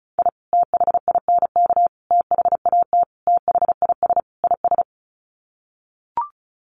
Morse code, heard as a keyed tone, about 34 wpm, 700 Hz. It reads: I THINK THAT THIS IS
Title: Sets of 5 Words Encore2 35wpm